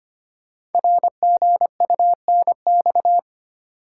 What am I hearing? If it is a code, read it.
Morse code, heard as a keyed tone, about 25 wpm, 700 Hz. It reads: RGUNX
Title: Code Groups - 5 Letters - 25wpm